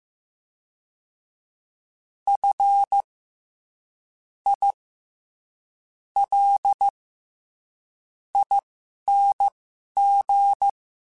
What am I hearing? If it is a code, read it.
Morse code, heard as a keyed tone, about 15 wpm, spaced as for 11 wpm, 800 Hz. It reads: F I L ING
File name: Special INGs-mp3s / Special INGs-mp3s / SlowToFaster ing @10